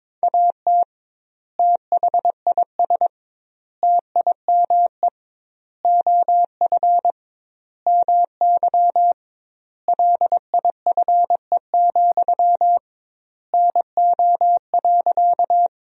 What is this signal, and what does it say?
Morse code, heard as a keyed tone, about 22 wpm, 700 Hz. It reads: AT THIS TIME OF MY LIFE, NO.
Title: 2x Word-Spacing - Sentences from Top 300 Words - 22wpm